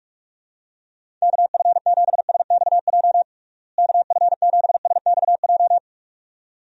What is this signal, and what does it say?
Morse code, heard as a keyed tone, about 46 wpm, 700 Hz. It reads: KF7SXJ KF7SXJ
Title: USA Call Signs - 6 Characters - Repeated 2x - 45wpm